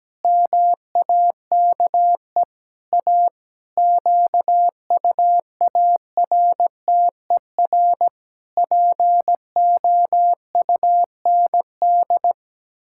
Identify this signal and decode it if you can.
Morse code, heard as a keyed tone, about 17 wpm, 700 Hz. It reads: MAKE A QUARTER POUND
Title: Pangram Challenge v2 - 17wpm